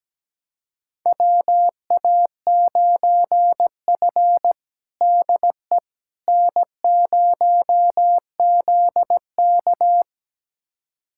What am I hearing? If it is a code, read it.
Morse code, heard as a keyed tone, about 17 wpm, 700 Hz. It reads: WA9F DE N0ZK
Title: USA Callsign DE Callsign - 4 Characters - 17wpm